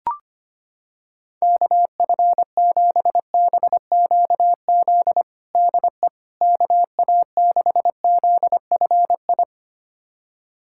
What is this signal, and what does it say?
Morse code, heard as a keyed tone, about 25 wpm, 700 Hz. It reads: KF7BQZ DE KA6ZFI
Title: USA Callsign DE Callsign - 6 Characters - 25wpm